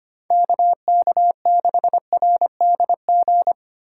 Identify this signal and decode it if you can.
Morse code, heard as a keyed tone, about 32 wpm, 700 Hz. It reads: KK6RDG